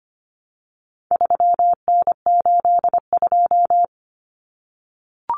Morse code, heard as a keyed tone, about 25 wpm, 700 Hz. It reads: 3N82
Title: Common Cut Numbers - 4-Digit 25wpm